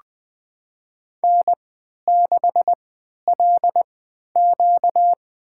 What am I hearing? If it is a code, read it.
Morse code, heard as a keyed tone, about 20 wpm, spaced as for 11 wpm, 700 Hz. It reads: N6LQ
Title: USA Call Signs - 4 Characters - 3x Character Spacing - 20wpm